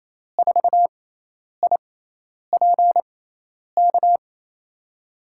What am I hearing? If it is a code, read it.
Morse code, heard as a keyed tone, about 27 wpm, spaced as for 10 wpm, 700 Hz. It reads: 4IPK
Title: USA Call Signs - 6 Characters - 6x Character Spacing - 28wpm